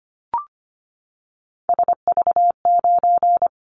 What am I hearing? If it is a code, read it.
Morse code, heard as a keyed tone, about 25 wpm, 700 Hz. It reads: SV9
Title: Callsign Prefixes for Top 100 DXCC - 25wpm